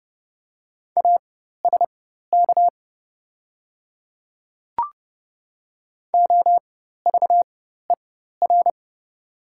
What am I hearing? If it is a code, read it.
Morse code, heard as a keyed tone, about 30 wpm, spaced as for 14 wpm, 700 Hz. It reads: ASK OVER
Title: 4x Character Spacing - Top 300 Words - 30wpm